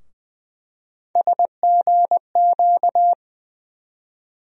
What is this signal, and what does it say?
Morse code, heard as a keyed tone, about 20 wpm, 700 Hz. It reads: SGQ
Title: USA Call Sign Postfixes - 3 Letters 20wpm